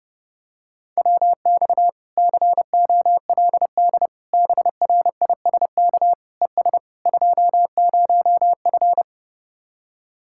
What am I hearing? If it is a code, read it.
Morse code, heard as a keyed tone, about 30 wpm, 700 Hz. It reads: WX COLD BRISK ES 20F